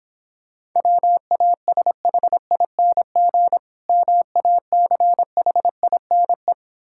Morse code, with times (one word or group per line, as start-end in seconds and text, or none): 0.8-3.6 WASHING
3.9-6.5 MACHINE